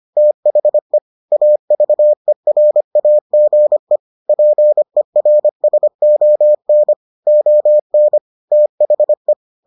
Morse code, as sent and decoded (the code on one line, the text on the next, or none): - .... / .- ...- . .-. .- --. / .--. .-. ... --- -. / --- -. / - .... .
THE AVERAGE PERSON ON THE